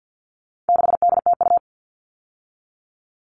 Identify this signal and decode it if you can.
Morse code, heard as a keyed tone, about 50 wpm, 700 Hz. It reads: 6DTU
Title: All Cut Numbers - 4-Digit 50wpm